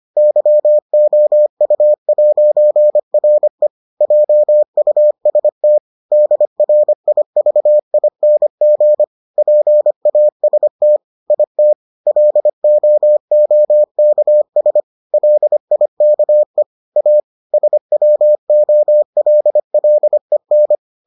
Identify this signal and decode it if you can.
Morse code, heard as a keyed tone, about 25 wpm, 600 Hz. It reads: YOU'RE JUST DRIVING PAST IT LOOKS LIKE A SWOLLEN